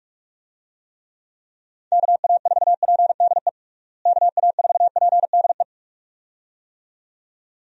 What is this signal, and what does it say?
Morse code, heard as a keyed tone, about 41 wpm, 700 Hz. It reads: KA4PDE KA4PDE
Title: USA Call Signs - 6 Characters - Repeated 2x - 45wpm